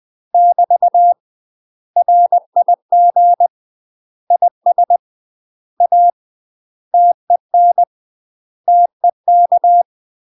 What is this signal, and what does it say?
Morse code, heard as a keyed tone, about 20 wpm, 700 Hz. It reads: <BT> RIG IS A TEN TEK
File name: UK MASTER QSOs TRAINING/20WPM